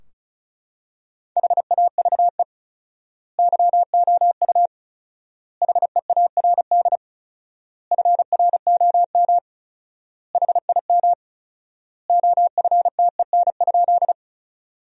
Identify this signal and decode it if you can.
Morse code, heard as a keyed tone, about 36 wpm, 700 Hz. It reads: HAVE YOU HEARD FROM HIM OFTEN?
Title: 4x Word-Spacing - Sentences from Top 400 Words - 35wpm